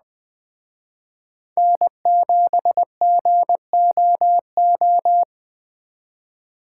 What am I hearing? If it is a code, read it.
Morse code, heard as a keyed tone, about 20 wpm, 700 Hz. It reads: N7GOO